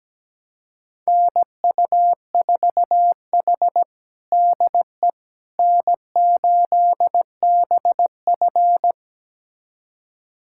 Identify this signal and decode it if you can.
Morse code, heard as a keyed tone, about 17 wpm, 700 Hz. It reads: NU4H DE N8BF
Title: USA Callsign DE Callsign - 4 Characters - 17wpm